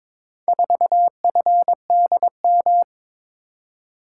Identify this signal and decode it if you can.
Morse code, heard as a keyed tone, about 22 wpm, 700 Hz. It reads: 4FDM
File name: practice / 22wpm / Four Letter-Number Combinations - 22wpm